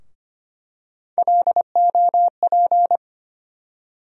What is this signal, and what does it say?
Morse code, heard as a keyed tone, about 25 wpm, 700 Hz. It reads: LOP
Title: USA Call Sign Postfixes - 3 Letters 25wpm